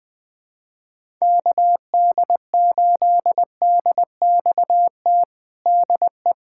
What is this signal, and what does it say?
Morse code, heard as a keyed tone, about 20 wpm, 700 Hz. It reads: KD8DXT DE
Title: USA Callsign DE Callsign - 6 Characters - 20wpm